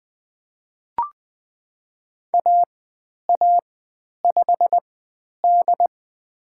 Morse code, heard as a keyed tone, about 20 wpm, spaced as for 10 wpm, 700 Hz. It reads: AA5D
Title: 102 Lesson 17g - Letter u - Call Signs 20wpm